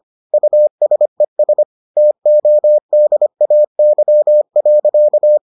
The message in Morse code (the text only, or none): USES TODAY.